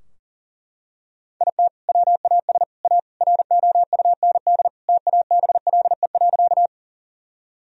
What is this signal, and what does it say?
Morse code, heard as a keyed tone, about 40 wpm, 700 Hz. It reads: IT WAS A ROUND TABLE.
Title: Sentences from Top 1000 Words - Review Part 2 - 40wpm